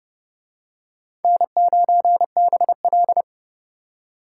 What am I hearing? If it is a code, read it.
Morse code, heard as a keyed tone, about 28 wpm, 700 Hz. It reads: N9BL